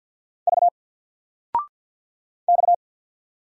Morse code, heard as a keyed tone, about 48 wpm, 700 Hz. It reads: V X
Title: Single Letter-Number Alt 50wpm